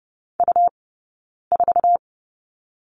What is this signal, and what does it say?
Morse code, heard as a keyed tone, about 30 wpm, 700 Hz. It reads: U 4
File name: Lesson 17d - Letter u - Two Characters 30wpm